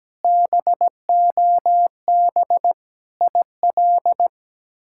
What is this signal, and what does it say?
Morse code, heard as a keyed tone, about 17 wpm, 700 Hz. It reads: BOB IL
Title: K1USN SST QSO Practice - 17wpm